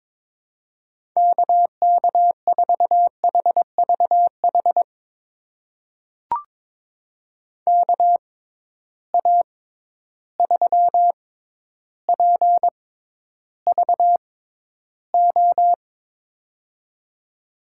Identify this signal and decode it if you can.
Morse code, heard as a keyed tone, about 22 wpm, spaced as for 11 wpm, 700 Hz. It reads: KK4HVH KA3PVO
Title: USA Call Signs - 6 Characters - 6x Character Spacing - 22wpm